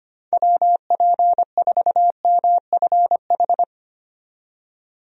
Morse code, heard as a keyed tone, about 25 wpm, 700 Hz. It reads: WP4MFH
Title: USA Call Signs - 6 Characters - 25wpm